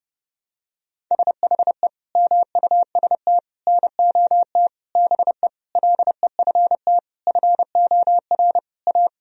Morse code, heard as a keyed tone, about 32 wpm, 700 Hz. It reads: SHE MUST NOT BE LEFT FOR A
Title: Sentences from Top 1000 Words - Review Part 1 - 30wpm